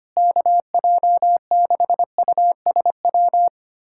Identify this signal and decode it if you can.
Morse code, heard as a keyed tone, about 25 wpm, 700 Hz. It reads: KJ6USW
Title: USA Callsign DE Callsign - 6 Characters - 25wpm